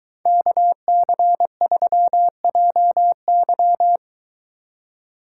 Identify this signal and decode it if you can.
Morse code, heard as a keyed tone, about 22 wpm, 700 Hz. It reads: KC3JY